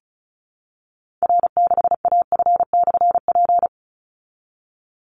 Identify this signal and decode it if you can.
Morse code, heard as a keyed tone, about 34 wpm, 700 Hz. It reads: R6AF/P
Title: Call Signs Repeated 3x - 35wpm